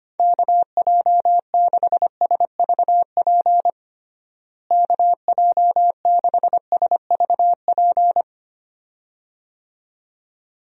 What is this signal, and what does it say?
Morse code, heard as a keyed tone, about 25 wpm, 700 Hz. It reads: KJ6SVP KJ6SVP
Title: USA Call Signs - 6 Characters - Repeated 2x - 25wpm